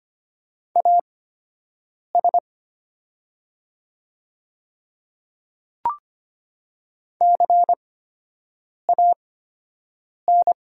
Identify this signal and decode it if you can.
Morse code, heard as a keyed tone, about 25 wpm, spaced as for 7 wpm, 700 Hz. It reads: AS CAN
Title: 8x Character Spacing - Top 100 Words - 25wpm